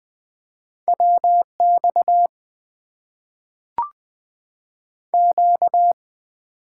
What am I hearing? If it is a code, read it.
Morse code, heard as a keyed tone, about 20 wpm, 700 Hz. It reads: WX Q
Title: Top 90 QSO Elements - 8x Character Spacing - 20wpm